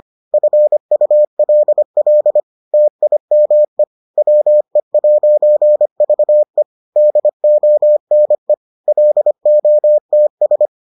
Morse code, heard as a keyed tone, about 25 wpm, 600 Hz. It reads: FULL TIME WE'VE DONE LOTS